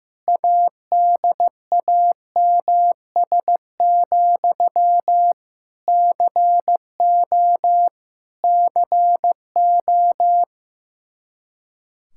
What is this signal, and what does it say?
Morse code, heard as a keyed tone, about 16 wpm, 700 Hz. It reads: ADAMS, CO CO
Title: Practice QSO - Exchange 1 - Beginning - 15wpm